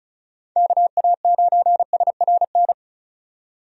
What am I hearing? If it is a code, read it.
Morse code, heard as a keyed tone, about 34 wpm, 700 Hz. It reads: KA9SRN
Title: USA Call Signs - 6 Characters - 35wpm